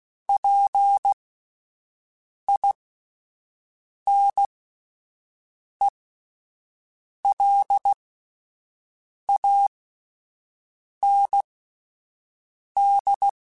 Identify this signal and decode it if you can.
Morse code, heard as a keyed tone, about 16 wpm, spaced as for 5 wpm, 800 Hz. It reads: PINELAND